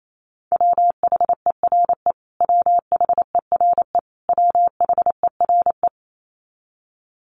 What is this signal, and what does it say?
Morse code, heard as a keyed tone, about 28 wpm, 700 Hz. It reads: WHERE WHERE WHERE
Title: Top 500 Words Repeated 28wpm